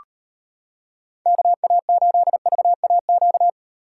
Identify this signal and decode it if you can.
Morse code, heard as a keyed tone, about 33 wpm, 700 Hz. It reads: KA8VAQ